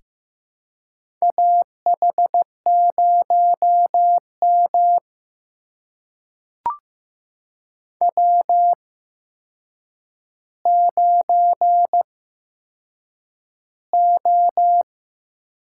Morse code, heard as a keyed tone, about 15 wpm, spaced as for 7 wpm, 700 Hz. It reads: AH0M W9O
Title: USA Call Signs - 4 Characters - 8x Character Spacing - 15wpm